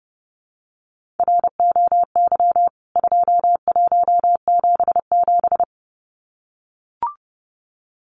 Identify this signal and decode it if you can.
Morse code, heard as a keyed tone, about 30 wpm, 700 Hz. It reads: ROY 2177
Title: CWT Exchange 30wpm